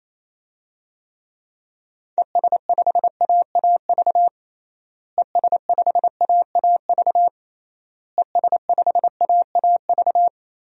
Morse code, heard as a keyed tone, about 28 wpm, 700 Hz. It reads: ES5AAV ES5AAV ES5AAV
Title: DXCC-100 Call Signs - 6 Characters - Repeated 3x - 28wpm